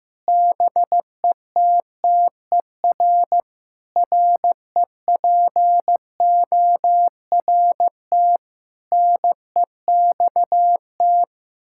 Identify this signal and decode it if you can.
Morse code, heard as a keyed tone, about 16 wpm, 700 Hz. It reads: BETTER REPORT NEXT
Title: Sentences from Top 1000 Words - Review Part 6 - 15wpm